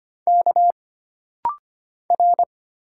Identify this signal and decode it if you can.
Morse code, heard as a keyed tone, about 25 wpm, 700 Hz. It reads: K R
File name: Single Character (Letter:Number:Punctuation:Prosign) - Rapid-Fire - 25wpm